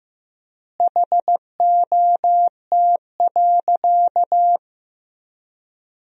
Morse code, heard as a keyed tone, about 15 wpm, 700 Hz. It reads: HOT.
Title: 3x Word-Spacing - Sentences from Top 500 Words - 15wpm